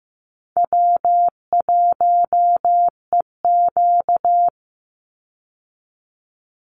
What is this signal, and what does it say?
Morse code, heard as a keyed tone, about 15 wpm, 700 Hz. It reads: W1EQ
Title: Easy Call Signs Repeated 3x - 15wpm